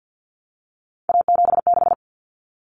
Morse code, heard as a keyed tone, about 50 wpm, 700 Hz. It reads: A76
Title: Common Cut Numbers - 3-Digit 50wpm